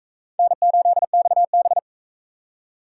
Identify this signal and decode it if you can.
Morse code, heard as a keyed tone, about 43 wpm, 700 Hz. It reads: N8XB